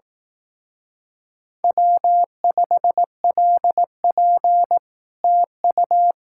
Morse code, heard as a keyed tone, about 18 wpm, 700 Hz. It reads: W5LP TU
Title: Speed-Racing Common QSO Phrases Encore - 15wpm